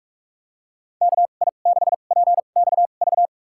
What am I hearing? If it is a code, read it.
Morse code, heard as a keyed tone, about 47 wpm, spaced as for 33 wpm, 700 Hz. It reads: KI6PXV
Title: USA Call Signs - 6 Characters - 2x Character Spacing - 45wpm